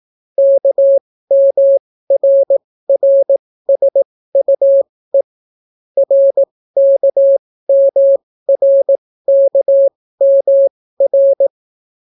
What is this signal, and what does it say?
Morse code, heard as a keyed tone, about 18 wpm, spaced as for 14 wpm, 550 Hz. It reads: KMRRSUE RKMRKMR